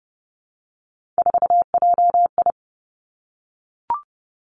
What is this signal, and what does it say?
Morse code, heard as a keyed tone, about 30 wpm, 700 Hz. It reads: 4JI
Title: Three Challenging Letter-Number Combinations 30wpm